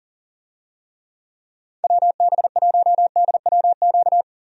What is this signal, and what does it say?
Morse code, heard as a keyed tone, about 40 wpm, 700 Hz. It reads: WB1DWQ
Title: USA Call Signs - 6 Characters - 40wpm